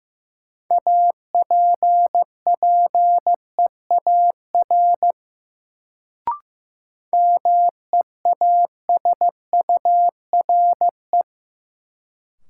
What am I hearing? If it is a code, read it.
Morse code, heard as a keyed tone, about 15 wpm, 700 Hz. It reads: APPEAR MEASURE
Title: Sentences from Top 1200 Words - New Word Drill Alt - 15wpm